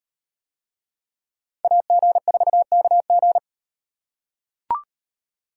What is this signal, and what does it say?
Morse code, heard as a keyed tone, about 38 wpm, 700 Hz. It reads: AG4KG